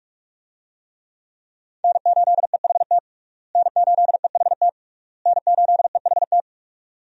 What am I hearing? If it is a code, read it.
Morse code, heard as a keyed tone, about 42 wpm, 700 Hz. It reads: N8EHT N8EHT N8EHT
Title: USA Call Signs - 5 Characters - Repeated 3x - 45wpm